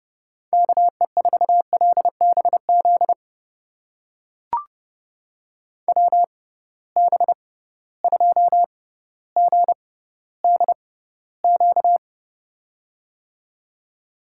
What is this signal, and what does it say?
Morse code, heard as a keyed tone, about 30 wpm, spaced as for 15 wpm, 700 Hz. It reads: KE4LBZ WB2GDQ
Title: USA Call Signs - 6 Characters - 6x Character Spacing - 30wpm